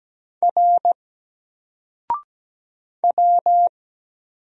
Morse code, heard as a keyed tone, about 17 wpm, 700 Hz. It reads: R W
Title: Single Character (Letter:Number:Punctuation:Prosign) Alt 17wpm